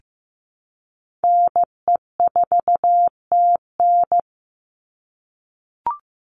Morse code, heard as a keyed tone, about 15 wpm, 700 Hz. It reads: NE4TN
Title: NA SOTA Chaser Callsigns 15wpm